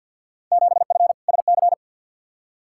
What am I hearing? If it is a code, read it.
Morse code, heard as a keyed tone, about 52 wpm, 700 Hz. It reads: 7F SC
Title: Field Day Exchanges Encore - 50wpm